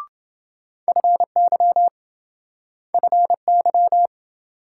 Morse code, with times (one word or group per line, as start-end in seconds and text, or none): 0.9-1.9 FY
2.9-4.1 FY